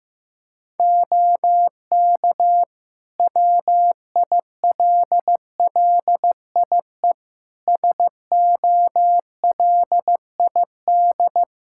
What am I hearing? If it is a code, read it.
Morse code, heard as a keyed tone, about 16 wpm, 700 Hz. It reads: OK WILLIE SOLID
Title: Practice QSO - Exchange 3 - More Info - 15wpm